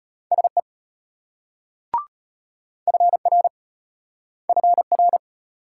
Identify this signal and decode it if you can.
Morse code, heard as a keyed tone, about 34 wpm, 700 Hz. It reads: SE FR FR